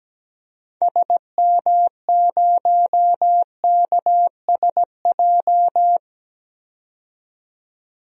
Morse code, heard as a keyed tone, about 17 wpm, 700 Hz. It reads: SM0KSJ
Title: DXCC-100 Call Signs - 6 Characters - 17wpm